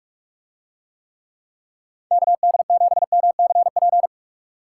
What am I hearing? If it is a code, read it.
Morse code, heard as a keyed tone, about 44 wpm, 700 Hz. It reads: KD7MCP